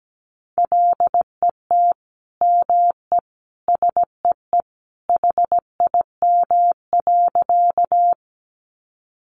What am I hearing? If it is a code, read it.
Morse code, heard as a keyed tone, about 17 wpm, 700 Hz. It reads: LET ME SEE HIM.
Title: Sentences from Top 300 Words 17wpm